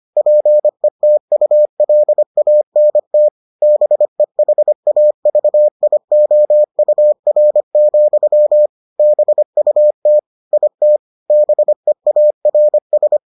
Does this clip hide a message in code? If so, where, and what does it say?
Yes: Morse code from 0.2 to 13.2 s, heard as a keyed tone, about 25 wpm, 600 Hz. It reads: PETULANT BEHAVIOUR, BUT IT BEARS